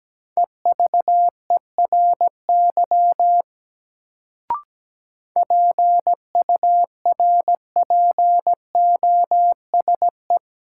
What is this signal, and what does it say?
Morse code, heard as a keyed tone, about 17 wpm, 700 Hz. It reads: EVERY PURPOSE
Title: Sets of 5 Words - New Word Drill - 17wpm